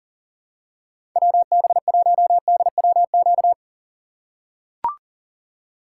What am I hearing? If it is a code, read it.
Morse code, heard as a keyed tone, about 40 wpm, 700 Hz. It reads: WB1DWQ
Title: USA Call Signs - 6 Characters - 3x Character Spacing - 40wpm